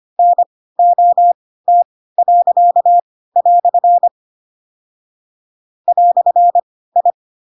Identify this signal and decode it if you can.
Morse code, heard as a keyed tone, about 25 wpm, spaced as for 16 wpm, 700 Hz. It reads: NOT." "I
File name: Pinocchio / Pinocchio ch2 10 25 ik1hge